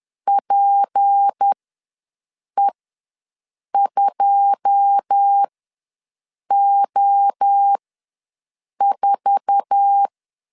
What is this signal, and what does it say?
Morse code, heard as a keyed tone, about 10 wpm, spaced as for 6 wpm, 800 Hz. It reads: PE2O4
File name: LEZIONI / 10 SIGNS / 10B INT ws 10wpm 4134